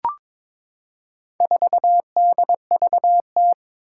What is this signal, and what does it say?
Morse code, heard as a keyed tone, about 23 wpm, 700 Hz. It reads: 4DVT
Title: Four Challenging Letter-Number Combinations 22wpm